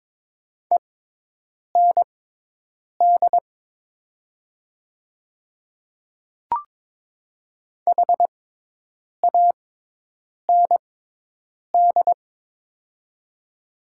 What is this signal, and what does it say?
Morse code, heard as a keyed tone, about 22 wpm, spaced as for 8 wpm, 700 Hz. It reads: END HAND
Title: 6x Character Spacing - Top 200 Words - 22wpm